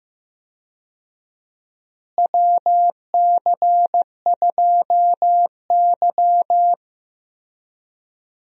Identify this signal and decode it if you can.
Morse code, heard as a keyed tone, about 15 wpm, 700 Hz. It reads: WC2Y
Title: USA Call Signs - 4 Characters - Repeated 2x - 15wpm